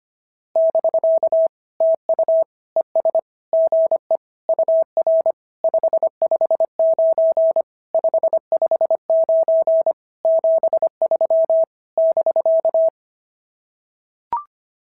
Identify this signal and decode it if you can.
Morse code, heard as a keyed tone, about 24 wpm, 650 Hz. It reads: <BK> TU ES GE UR 559 559 73 <BK>
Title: SOTA QSO Practice 25wpm